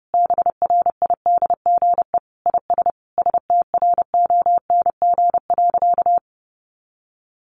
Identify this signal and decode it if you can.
Morse code, heard as a keyed tone, about 30 wpm, 700 Hz. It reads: BRIDGE IS STRONG.
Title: Sentences from Top 1400 Words 30wpm